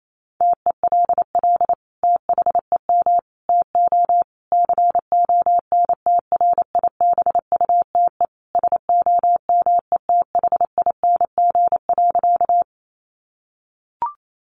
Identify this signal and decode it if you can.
Morse code, heard as a keyed tone, about 28 wpm, 700 Hz. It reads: TELL THEM TO CONTRIBUTE SOMETHING.
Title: Sentences from Top 1400 Words 28wpm